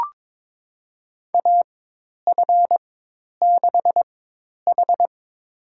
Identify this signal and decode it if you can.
Morse code, heard as a keyed tone, about 22 wpm, spaced as for 10 wpm, 700 Hz. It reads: AF6H
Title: USA Call Signs - 4x Character Spacing - 22wpm